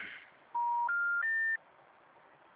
Special information tone.
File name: out- -2001- - - .523